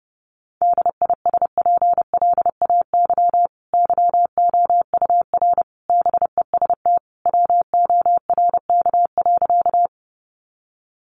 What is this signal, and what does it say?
Morse code, heard as a keyed tone, about 30 wpm, 700 Hz. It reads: DISPLAY YOUR BEST WORK.
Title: Sentences from Top 1200 Words 30wpm